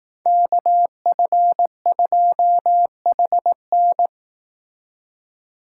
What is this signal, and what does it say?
Morse code, heard as a keyed tone, about 18 wpm, 700 Hz. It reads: KF2HN
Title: Speed Racing - USA 2x2 Call Signs 15wpm